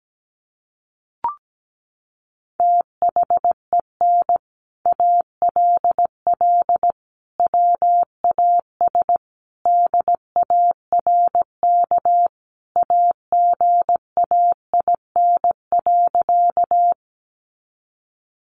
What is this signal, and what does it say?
Morse code, heard as a keyed tone, about 17 wpm, 700 Hz. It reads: THEN ALL WAS DARK AGAIN.
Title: Sentences from Top 500 Words 17wpm